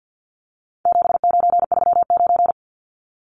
Morse code, heard as a keyed tone, about 48 wpm, 700 Hz. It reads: 7939
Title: Numbers - 4-Digit 50wpm